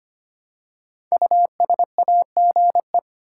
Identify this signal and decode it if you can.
Morse code, heard as a keyed tone, about 25 wpm, 700 Hz. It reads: USAGE
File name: 5 Letter Words - First 3 Letters Given - 25wpm